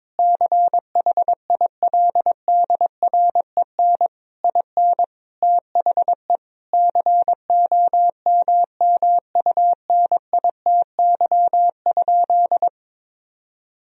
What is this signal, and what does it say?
Morse code, heard as a keyed tone, about 21 wpm, 700 Hz. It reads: CHILDREN IN THE COMMUNITY?
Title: Sentences from Top 1000 Words - Review Part 4 - 22wpm